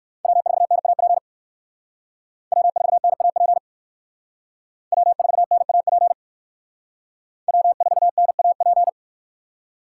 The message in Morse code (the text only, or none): W4NAP W4NAP W4NAP W4NAP